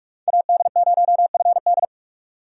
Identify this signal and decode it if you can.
Morse code, heard as a keyed tone, about 47 wpm, 700 Hz. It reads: AD0FD
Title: USA Call Signs - 5 Characters - 2x Character Spacing - 45wpm